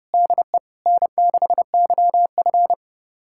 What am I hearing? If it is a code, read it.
Morse code, heard as a keyed tone, about 30 wpm, 700 Hz. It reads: DE N6YF
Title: USA Callsign DE Callsign - 4 Characters - 30wpm